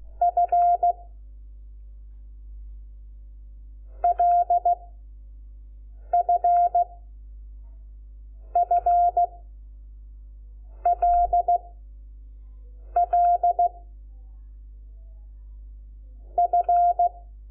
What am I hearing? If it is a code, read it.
Morse code, heard as a keyed tone, about 16 wpm, 700 Hz. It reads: F L F F L L F
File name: Lecciones / CLASE Nº5